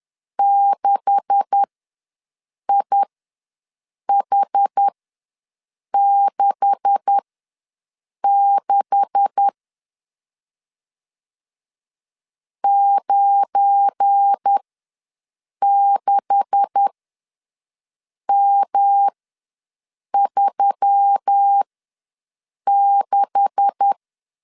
Morse code, heard as a keyed tone, about 10 wpm, spaced as for 6 wpm, 800 Hz. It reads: 6IH66 96M36
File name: LEZIONI / 9 286 ws 10wpm 4032